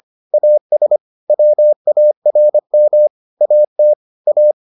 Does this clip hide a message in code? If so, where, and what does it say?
Yes: Morse code from 0.3 to 4.5 s, heard as a keyed tone, about 25 wpm, 600 Hz. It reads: AS WARM AT A